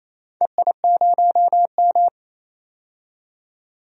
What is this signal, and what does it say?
Morse code, heard as a keyed tone, about 28 wpm, 700 Hz. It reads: EI0M
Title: DXCC-100 Call Signs - 4 Characters - 28wpm